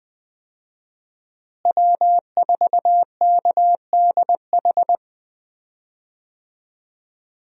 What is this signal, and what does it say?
Morse code, heard as a keyed tone, about 21 wpm, 700 Hz. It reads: W4KDH
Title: USA Call Signs - 5 Characters - Repeated 2x - 20wpm